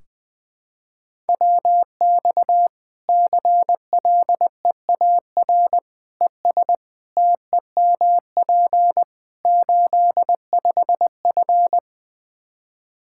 Morse code, heard as a keyed tone, about 19 wpm, 700 Hz. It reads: WX CLEAR ES TEMP 85F
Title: Practice QSO - Exchange 2 - Info - 20wpm